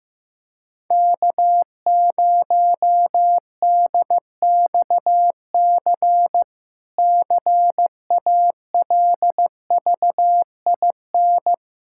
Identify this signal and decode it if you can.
Morse code, heard as a keyed tone, about 15 wpm, 700 Hz. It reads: K0DXC CALVIN
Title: K1USN SST QSO Practice - 15wpm